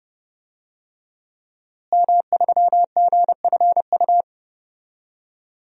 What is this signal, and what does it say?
Morse code, heard as a keyed tone, about 30 wpm, 700 Hz. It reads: M3GFU
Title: DXCC-100 Call Signs - 5 Characters - Repeated 2x - 30wpm